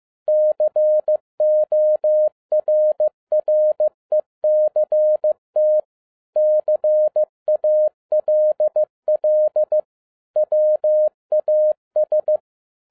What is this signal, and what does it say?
Morse code, heard as a keyed tone, about 15 wpm, 610 Hz. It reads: CORRECT CALL WAS
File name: 2011-10-mp3 / n07-A DOT 15wpm.mp3